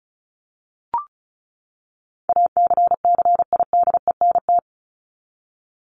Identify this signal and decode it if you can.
Morse code, heard as a keyed tone, about 34 wpm, 700 Hz. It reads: ACCIDENT
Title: Top 1751-2000 Words 35wpm